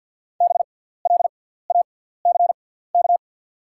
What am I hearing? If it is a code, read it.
Morse code, heard as a keyed tone, about 49 wpm, spaced as for 17 wpm, 690 Hz. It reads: BLACK
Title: 6x Character Spacing - Top 500 Words - 50wpm